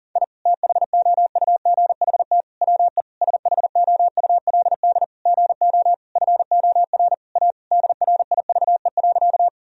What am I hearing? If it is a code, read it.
Morse code, heard as a keyed tone, about 40 wpm, 700 Hz. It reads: I THOUGHT WE SHOULD GO FOR A DRIVE.
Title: Sentences from Top 1000 Words - Review Part 2 - 40wpm